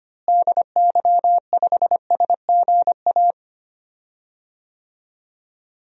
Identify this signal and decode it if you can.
Morse code, heard as a keyed tone, about 25 wpm, 700 Hz. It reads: DY5SGA
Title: DXCC-100 Call Signs - 6 Characters - Repeated 2x - 25wpm